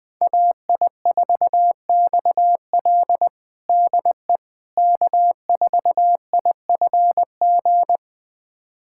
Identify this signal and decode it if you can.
Morse code, heard as a keyed tone, about 20 wpm, 700 Hz. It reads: AI4XL DE K4IFG
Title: USA Callsign DE Callsign - 5 Characters - 20wpm